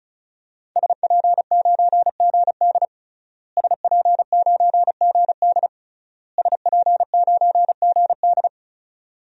Morse code, heard as a keyed tone, about 34 wpm, 700 Hz. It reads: SP9GD SP9GD SP9GD
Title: DXCC-100 Call Signs - 5 Characters - Repeated 3x - 35wpm